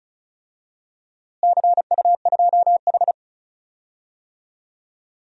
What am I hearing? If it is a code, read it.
Morse code, heard as a keyed tone, about 34 wpm, 700 Hz. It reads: CU2H
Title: DXCC-100 Call Signs - 4 Characters - Repeated 3x - 35wpm